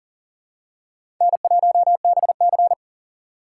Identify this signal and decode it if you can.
Morse code, heard as a keyed tone, about 38 wpm, 700 Hz. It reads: N1BC
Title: USA Call Signs - 6x Character Spacing - 40wpm